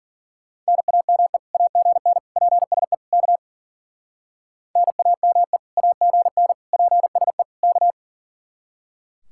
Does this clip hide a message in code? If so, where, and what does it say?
Yes: Morse code from 0.7 to 7.9 s, heard as a keyed tone, about 51 wpm, 700 Hz. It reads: NAME AGN PSE K NAME AGN PSE K